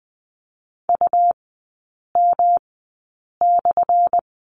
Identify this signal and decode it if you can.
Morse code, heard as a keyed tone, about 20 wpm, spaced as for 8 wpm, 700 Hz. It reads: UM/
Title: 178 Lesson 29e - Slash - Three Characters 20wpm